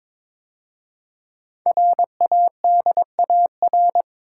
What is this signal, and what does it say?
Morse code, heard as a keyed tone, about 22 wpm, 700 Hz. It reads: RADAR
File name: 5 Letter Words - First 3 Letters Given - 22wpm